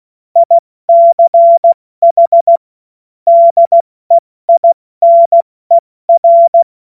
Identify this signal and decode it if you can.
Morse code, heard as a keyed tone, about 16 wpm, 680 Hz. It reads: ICH DEINER
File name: Morsekurs / Witze / Witz 09 Januar 2024 70 F80